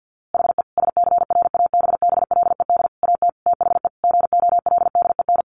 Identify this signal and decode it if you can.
Morse code, heard as a keyed tone, about 48 wpm, 700 Hz. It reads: HE SCRABBLED AT THE GOLDEN
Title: A Sound Of Thunder 50wpm